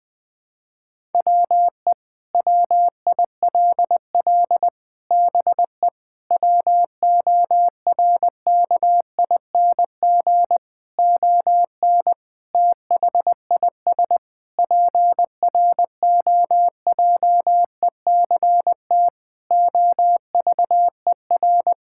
Morse code, heard as a keyed tone, about 20 wpm, 700 Hz. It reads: WE WILL BE WORKING ON THIS PROJECT OVER
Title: Sentences with 6-Word Phrases - 20wpm